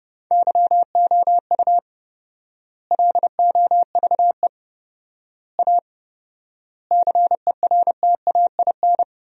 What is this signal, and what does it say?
Morse code, heard as a keyed tone, about 30 wpm, 700 Hz. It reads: YOU LOVE A CERTAIN
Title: 4x Word-Spacing - Sentences from Top 500 Words - 30wpm